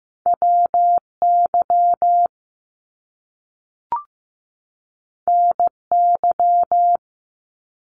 Morse code, heard as a keyed tone, about 15 wpm, 700 Hz. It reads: WY NY
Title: US States 15wpm